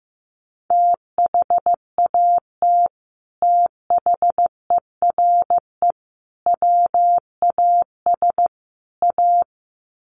Morse code, heard as a keyed tone, about 15 wpm, 700 Hz. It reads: THAT THERE WAS A
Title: Sets of 4 Words 15wpm